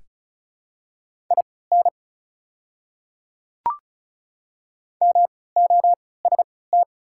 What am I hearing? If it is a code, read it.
Morse code, heard as a keyed tone, about 36 wpm, spaced as for 20 wpm, 700 Hz. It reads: IN MOST
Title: 3x Character Spacing - Top 100 Words - 35wpm